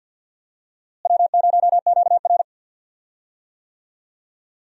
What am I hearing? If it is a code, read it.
Morse code, heard as a keyed tone, about 52 wpm, 700 Hz. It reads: W0QR
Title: USA Call Signs - 4 Characters - Repeated 2x - 50wpm